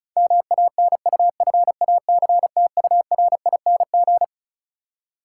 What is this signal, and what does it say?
Morse code, heard as a keyed tone, about 36 wpm, 700 Hz. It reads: MANUFACTURING
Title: Practice QSO - Exchange 3 - More Info - 35wpm